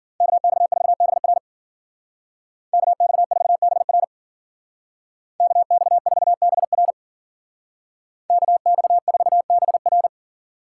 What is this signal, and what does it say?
Morse code, heard as a keyed tone, about 50 wpm, 700 Hz. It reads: KX4BR KX4BR KX4BR KX4BR
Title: Speed Racing - USA Call Signs 40wpm